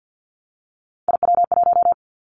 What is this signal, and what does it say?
Morse code, heard as a keyed tone, about 50 wpm, 700 Hz. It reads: IW1
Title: Callsign Prefixes for Top 50 DXCC - 50wpm